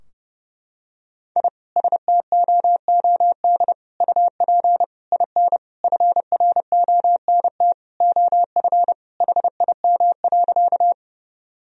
Morse code, heard as a keyed tone, about 30 wpm, 700 Hz. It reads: I STOOD UP IN FRONT OF HIM.
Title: Sentences from Top 1000 Words - Review Part 2 - 30wpm